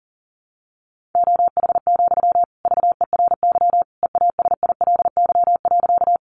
Morse code, heard as a keyed tone, about 40 wpm, 700 Hz. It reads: OH, VERY EASILY.
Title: Sentences from Top 900 Words 40wpm